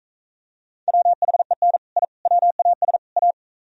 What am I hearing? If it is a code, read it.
Morse code, heard as a keyed tone, about 45 wpm, 700 Hz. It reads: WHEN I WAS A